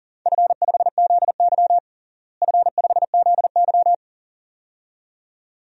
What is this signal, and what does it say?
Morse code, heard as a keyed tone, about 40 wpm, 700 Hz. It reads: F5ZY F5ZY
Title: DXCC-100 Call Signs - 4 Characters - Repeated 2x - 40wpm